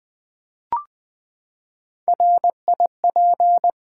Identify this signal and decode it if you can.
Morse code, heard as a keyed tone, about 21 wpm, 700 Hz. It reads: RIP
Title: Three Letter Words - 20wpm